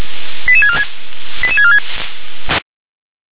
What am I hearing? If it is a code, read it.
Five-tone selective call (ZVEI), heard as quick runs of five tones, tones of about 70 ms. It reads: 88646 88646